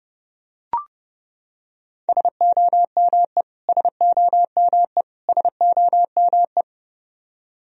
Morse code, heard as a keyed tone, about 32 wpm, 700 Hz. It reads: SOME SOME SOME
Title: Top 100 Words Repeated 30wpm